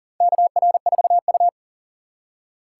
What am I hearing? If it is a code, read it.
Morse code, heard as a keyed tone, about 40 wpm, 700 Hz. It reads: KR4U
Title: USA Call Signs - 4 Characters - 3x Character Spacing - 40wpm